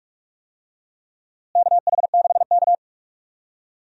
Morse code, heard as a keyed tone, about 44 wpm, 700 Hz. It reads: KH6K
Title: DXCC-100 Call Signs - 4 Characters - Repeated 3x - 45wpm